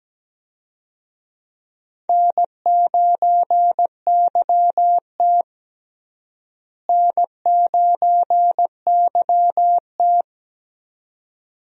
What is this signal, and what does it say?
Morse code, heard as a keyed tone, about 17 wpm, 700 Hz. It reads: N9YT N9YT
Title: USA Call Signs - 4 Characters - Repeated 2x - 17wpm